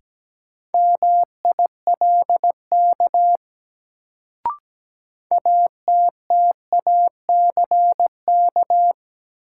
Morse code, heard as a keyed tone, about 17 wpm, 700 Hz. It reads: MILK ATTACK
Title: Sentences from Top 1500 Words - New Word Drill - 17wpm